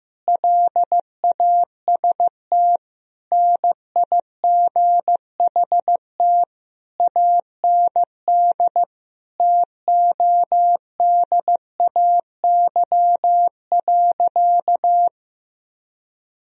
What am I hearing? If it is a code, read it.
Morse code, heard as a keyed tone, about 15 wpm, 700 Hz. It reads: LAST NIGHT AND TODAY.
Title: Sentences from Top 1000 Words - Review Part 4 - 15wpm